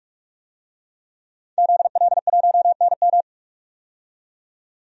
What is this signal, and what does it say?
Morse code, heard as a keyed tone, about 42 wpm, 700 Hz. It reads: ZL1NM
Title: DXCC-100 Call Signs - 5 Characters - 45wpm